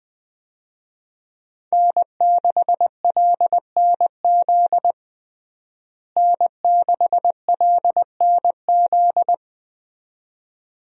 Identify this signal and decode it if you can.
Morse code, heard as a keyed tone, about 21 wpm, 700 Hz. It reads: N6LNZ N6LNZ
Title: USA Call Signs - 5 Characters - Repeated 2x - 20wpm